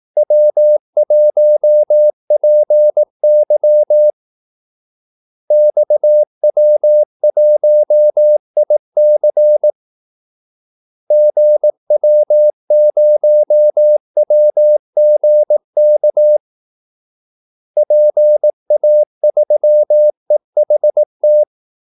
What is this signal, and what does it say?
Morse code, heard as a keyed tone, about 18 wpm, 600 Hz. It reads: W1PY XW1IC GW0WGK PA3EHT
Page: Kaldesignaler 18 wpm